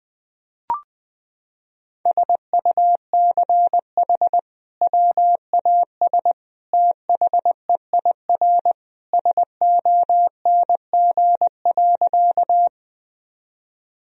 Morse code, heard as a keyed tone, about 21 wpm, 700 Hz. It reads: SUCH WAS THEIR SONG.
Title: Sentences from Top 1000 Words - Review Part 2 - 20wpm